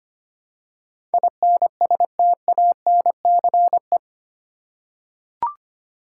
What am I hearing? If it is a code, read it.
Morse code, heard as a keyed tone, about 25 wpm, 700 Hz. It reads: INSTANCE
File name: Eight Letter Words - 25wpm